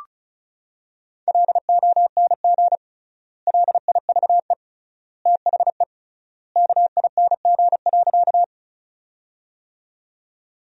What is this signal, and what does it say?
Morse code, heard as a keyed tone, about 36 wpm, 700 Hz. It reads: LONG LIVE THE KING.
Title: 3x Word-Spacing - Sentences from Top 400 Words - 35wpm